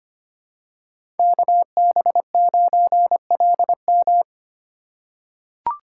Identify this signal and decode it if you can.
Morse code, heard as a keyed tone, about 25 wpm, 700 Hz. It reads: KB9LM
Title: USA Call Signs - 5 Characters - 2x Character Spacing - 25wpm